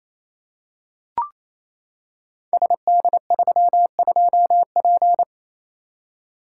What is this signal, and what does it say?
Morse code, heard as a keyed tone, about 29 wpm, 700 Hz. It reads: SD32P
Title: Code Groups - 5 Letter:Numbers - 28wpm